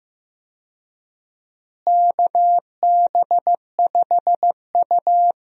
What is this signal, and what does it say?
Morse code, heard as a keyed tone, about 16 wpm, 700 Hz. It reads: KB5U
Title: USA Call Signs - 4 Characters - Repeated 3x - 15wpm